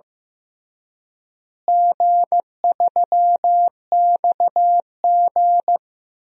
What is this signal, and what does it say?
Morse code, heard as a keyed tone, about 16 wpm, 700 Hz. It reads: G3XG
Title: DXCC-100 Call Signs - 4 Characters - Repeated 3x - 15wpm